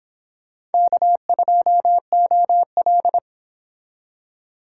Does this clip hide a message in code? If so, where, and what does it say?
Yes: Morse code from 0.7 to 3.2 s, heard as a keyed tone, about 29 wpm, 700 Hz. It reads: K2OL